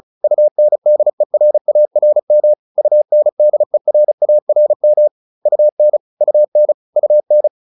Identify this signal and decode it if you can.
Morse code, heard as a keyed tone, about 34 wpm, 600 Hz. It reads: UNDERARM UNDERARM UN UN UN